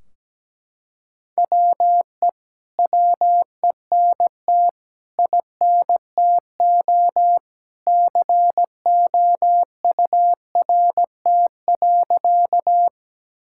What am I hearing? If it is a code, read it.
Morse code, heard as a keyed tone, about 17 wpm, 700 Hz. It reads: WE WENT INTO COURT.
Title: Sentences from Top 1600 Words 17wpm